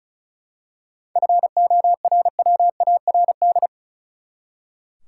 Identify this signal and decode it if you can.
Morse code, heard as a keyed tone, about 34 wpm, 700 Hz. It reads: FORWARD
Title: Seven Letter Words - 35wpm